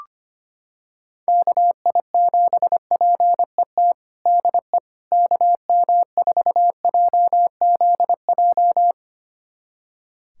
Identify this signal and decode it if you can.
Morse code, heard as a keyed tone, about 25 wpm, 700 Hz. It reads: KI7PET DE KM4JZJ
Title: USA Callsign DE Callsign - 6 Characters - 25wpm